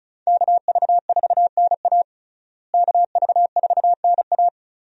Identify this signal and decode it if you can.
Morse code, heard as a keyed tone, about 34 wpm, 700 Hz. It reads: KV4NA KV4NA
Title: USA Call Signs - 5 Characters - Repeated 2x - 35wpm